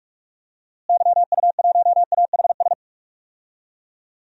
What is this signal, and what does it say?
Morse code, heard as a keyed tone, about 46 wpm, 700 Hz. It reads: YU1AHS
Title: DXCC-100 Call Signs - 6 Characters - Repeated 3x - 45wpm